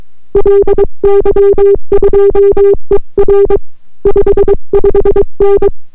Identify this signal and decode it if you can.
Morse code, heard as a keyed tone, about 22 wpm, 400 Hz. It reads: LY2ER 55N